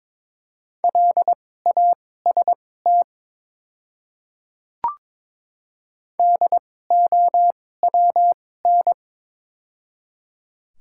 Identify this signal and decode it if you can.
Morse code, heard as a keyed tone, about 22 wpm, spaced as for 16 wpm, 700 Hz. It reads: LAST DOWN
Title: 2x Character Spacing - Top 500 Words - 22wpm